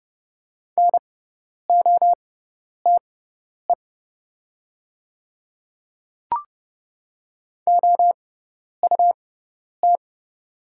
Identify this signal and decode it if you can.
Morse code, heard as a keyed tone, about 32 wpm, spaced as for 10 wpm, 700 Hz. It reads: NOTE OUT
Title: 6x Character Spacing - Top 500 Words - 30wpm